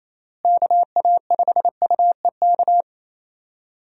Morse code, heard as a keyed tone, about 28 wpm, 700 Hz. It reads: KA5UEK
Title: USA Call Signs - 6 Characters - Repeated 3x - 28wpm